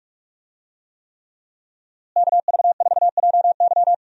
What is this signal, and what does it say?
Morse code, heard as a keyed tone, about 42 wpm, 700 Hz. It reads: KV4JY
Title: USA Call Signs - 5 Characters - Repeated 3x - 45wpm